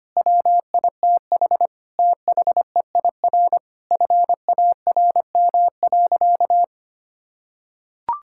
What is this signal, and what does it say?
Morse code, heard as a keyed tone, about 25 wpm, 700 Hz. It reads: WITH THEIR FARM.
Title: Sentences from Top 1000 Words - Review Part 6 - 25wpm